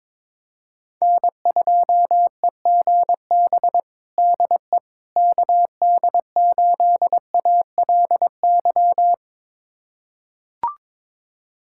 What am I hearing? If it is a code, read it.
Morse code, heard as a keyed tone, about 22 wpm, 700 Hz. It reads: N2EGB DE KD8ALY
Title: USA Callsign DE Callsign 22wpm